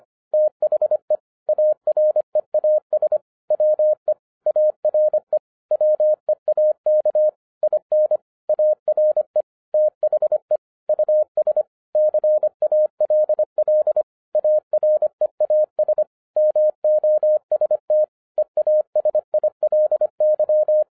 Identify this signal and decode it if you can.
Morse code, heard as a keyed tone, about 26 wpm, 610 Hz. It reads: THE AREAS WE ARE WEAK IN ARE THE US CALL AREAS MOST EASILY
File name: 2012-01-mp3 / n14-FUN WITH NUMBERS-25wpm